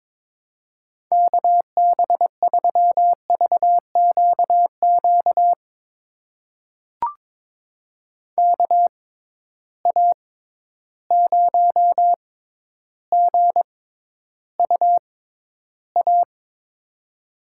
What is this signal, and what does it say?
Morse code, heard as a keyed tone, about 22 wpm, spaced as for 11 wpm, 700 Hz. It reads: KB3VQQ KA0GUA
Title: USA Call Signs - 6 Characters - 6x Character Spacing - 22wpm